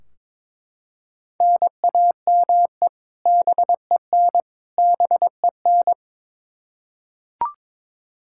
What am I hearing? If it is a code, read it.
Morse code, heard as a keyed tone, about 22 wpm, 700 Hz. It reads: NAME BEN BEN
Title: Practice QSO - Exchange 1 - Beginning - 22wpm